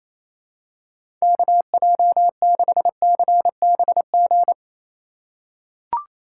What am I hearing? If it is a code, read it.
Morse code, heard as a keyed tone, about 27 wpm, 700 Hz. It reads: KJ6CBG
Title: USA Call Signs - 6 Characters - 4x Character Spacing - 28wpm